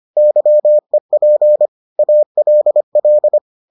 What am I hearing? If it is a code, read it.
Morse code, heard as a keyed tone, about 25 wpm, 600 Hz. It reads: YEP ALL